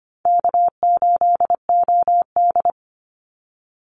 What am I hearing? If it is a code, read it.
Morse code, heard as a keyed tone, about 25 wpm, 700 Hz. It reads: K8OD
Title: Call Signs - 25wpm